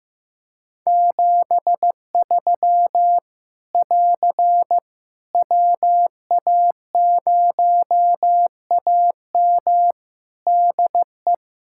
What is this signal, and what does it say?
Morse code, heard as a keyed tone, about 15 wpm, 700 Hz. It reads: 73 <AR> WA0AM DE
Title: K1USN SST QSO Practice 15wpm